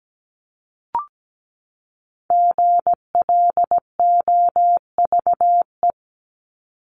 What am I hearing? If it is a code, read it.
Morse code, heard as a keyed tone, about 17 wpm, 700 Hz. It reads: GLOVE
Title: Top 1751-2000 Words 17wpm